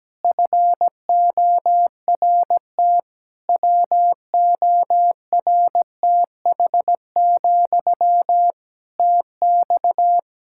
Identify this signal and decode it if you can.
Morse code, heard as a keyed tone, about 17 wpm, 700 Hz. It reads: FORT WORTH, TX
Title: Practice QSO - Exchange 1 - Beginning - 17wpm